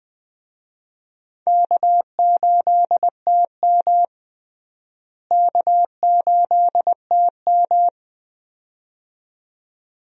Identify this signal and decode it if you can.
Morse code, heard as a keyed tone, about 20 wpm, 700 Hz. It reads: K8TM K8TM
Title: USA Call Signs - 4 Characters - Repeated 2x - 20wpm